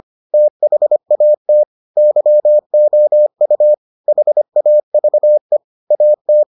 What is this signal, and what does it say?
Morse code, heard as a keyed tone, about 25 wpm, 600 Hz. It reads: THAT YOU HAVE AT